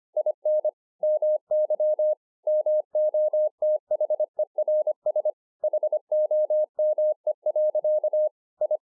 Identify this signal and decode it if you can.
Morse code, heard as a keyed tone, about 25 wpm, 610 Hz. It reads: IN MY MOTHERS HOME. I